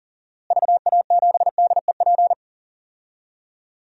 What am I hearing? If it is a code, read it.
Morse code, heard as a keyed tone, about 37 wpm, 700 Hz. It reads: VA7DEP
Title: DXCC-100 Call Signs - 6 Characters - Repeated 3x - 40wpm